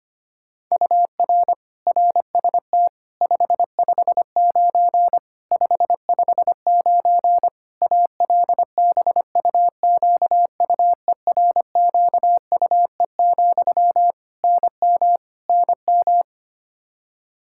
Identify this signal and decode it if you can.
Morse code, heard as a keyed tone, about 25 wpm, 700 Hz. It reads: UR RST 559 559 ALBUQUERQUE, NM NM
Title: Practice QSO - Exchange 1 - Beginning - 25wpm